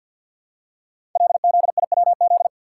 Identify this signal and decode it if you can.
Morse code, heard as a keyed tone, about 50 wpm, 700 Hz. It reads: LZIWZ
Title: Code Groups - 5 Letters - 50wpm